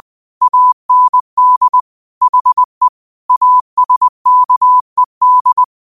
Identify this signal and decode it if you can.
Morse code, heard as a keyed tone, about 20 wpm, 1000 Hz. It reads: AND HE ASKED
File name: qotdfiles / 20wpm